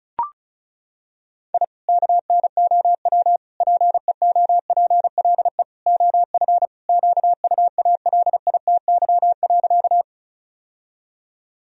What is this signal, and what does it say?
Morse code, heard as a keyed tone, about 36 wpm, 700 Hz. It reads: I KNOW PEOPLE OF QUALITY.
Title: Sentences from Top 1000 Words - Review Part 4 - 35wpm